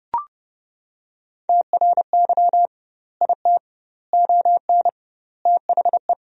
Morse code, heard as a keyed tone, about 30 wpm, 700 Hz. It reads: TRY IT ON THE
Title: 2x Word-Spacing - Sentences from Top 400 Words - 30wpm